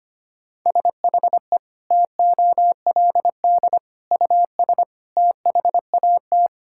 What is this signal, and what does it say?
Morse code, heard as a keyed tone, about 25 wpm, 700 Hz. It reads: SHE TOLD US THAT
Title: Sentences from Top 1000 Words - Review Part 6 - 25wpm